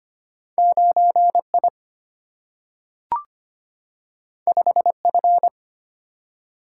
Morse code, heard as a keyed tone, about 25 wpm, 700 Hz. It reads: 9 I 5 F
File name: Two Letter-Number Combinations - 25wpm